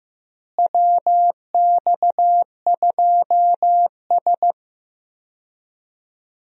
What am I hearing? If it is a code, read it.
Morse code, heard as a keyed tone, about 15 wpm, 700 Hz. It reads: WX2S
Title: USA Call Signs - 4 Characters - Repeated 3x - 15wpm